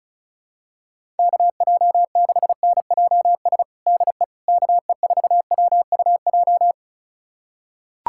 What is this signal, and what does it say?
Morse code, heard as a keyed tone, about 34 wpm, 700 Hz. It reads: KJ6NJS DE KE4WUJ
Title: USA Callsign DE Callsign 35wpm